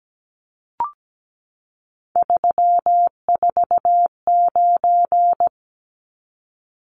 Morse code, heard as a keyed tone, about 17 wpm, 700 Hz. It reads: 349
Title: Numbers - 3-Digit 17wpm